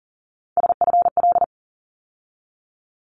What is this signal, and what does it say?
Morse code, heard as a keyed tone, about 38 wpm, 700 Hz. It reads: SFL
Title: ARRL Sections 40wpm